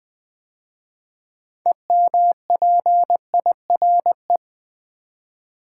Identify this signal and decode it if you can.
Morse code, heard as a keyed tone, about 20 wpm, 700 Hz. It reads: EMPIRE
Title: 6 Letter Words - First 3 Letters Given - 20wpm